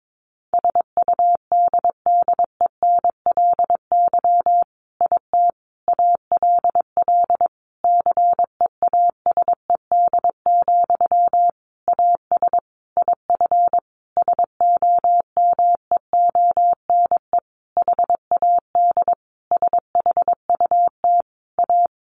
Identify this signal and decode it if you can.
Morse code, heard as a keyed tone, about 22 wpm, 700 Hz. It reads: SUDDENLY IT ALL CEASED, AS IF SOMEONE HAD SHUT A
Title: A Sound Of Thunder 22wpm